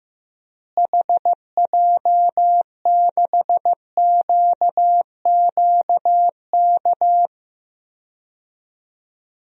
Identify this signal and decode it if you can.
Morse code, heard as a keyed tone, about 15 wpm, 700 Hz. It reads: HJ6QQK
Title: DXCC-100 Call Signs - 6 Characters - Repeated 3x - 15wpm